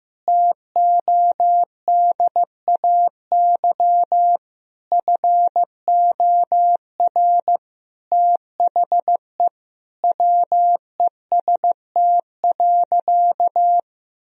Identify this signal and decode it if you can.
Morse code, heard as a keyed tone, about 16 wpm, 700 Hz. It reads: TODAY FOR THE WEST.
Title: Sentences from Top 1000 Words - Review Part 2 - 15wpm